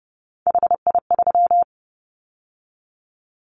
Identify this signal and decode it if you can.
Morse code, heard as a keyed tone, about 30 wpm, 700 Hz. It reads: HI3
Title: Callsign Prefixes for Top 100 DXCC - 30wpm